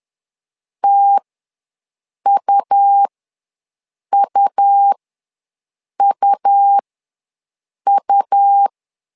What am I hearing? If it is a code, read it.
Morse code, heard as a keyed tone, about 11 wpm, spaced as for 6 wpm, 800 Hz. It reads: TUUUU
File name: LEZIONI / 2 UHDN / 2 U ws 10wpm 2017